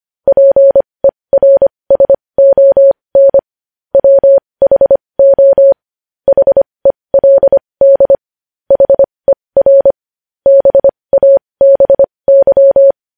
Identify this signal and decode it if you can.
Morse code, heard as a keyed tone, about 25 wpm, spaced as for 20 wpm, 550 Hz. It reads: PERSON WHO HELD HER BABY
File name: podcast / News headlines-1x-25x20